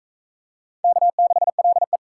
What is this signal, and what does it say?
Morse code, heard as a keyed tone, about 37 wpm, 700 Hz. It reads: K6LE